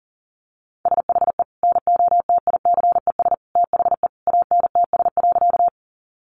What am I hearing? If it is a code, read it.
Morse code, heard as a keyed tone, about 40 wpm, 700 Hz. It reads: SHE NOTICES THE ANTS.
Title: Additional Easy Phrases 40wpm